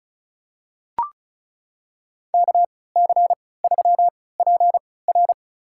Morse code, heard as a keyed tone, about 34 wpm, spaced as for 20 wpm, 700 Hz. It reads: KC3PR
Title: USA Call Signs - 3x Character Spacing - 35wpm